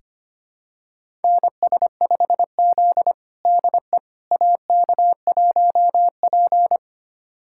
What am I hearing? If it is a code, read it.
Morse code, heard as a keyed tone, about 25 wpm, 700 Hz. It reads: NS5Z DE AK1P
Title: USA Callsign DE Callsign - 4 Characters - 25wpm